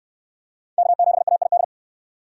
Morse code, heard as a keyed tone, about 69 wpm, 700 Hz. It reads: K7AID